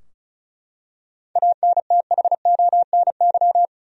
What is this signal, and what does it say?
Morse code, heard as a keyed tone, about 36 wpm, 700 Hz. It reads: ANTHONY
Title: USA Common Names - 35wpm